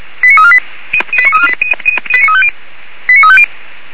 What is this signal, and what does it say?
Five-tone selective call (ZVEI), heard as quick runs of five tones, tones of about 70 ms. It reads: 89348 89350